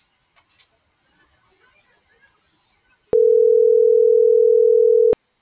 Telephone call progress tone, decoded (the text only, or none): ringback tone